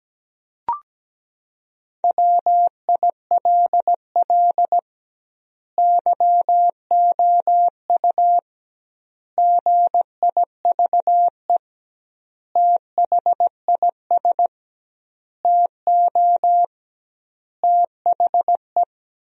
Morse code, heard as a keyed tone, about 17 wpm, 700 Hz. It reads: WILL YOU GIVE THIS TO THE
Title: 2x Word-Spacing - Sentences from Top 200 Words - 17wpm